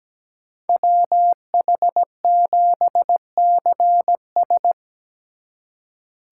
Morse code, heard as a keyed tone, about 20 wpm, 700 Hz. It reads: WH7CS
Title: Speed Racing - USA 2x2 Call Signs 17wpm